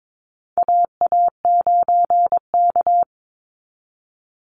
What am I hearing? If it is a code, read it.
Morse code, heard as a keyed tone, about 22 wpm, 700 Hz. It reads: AA9K
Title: Call Signs Repeated 3x - 22wpm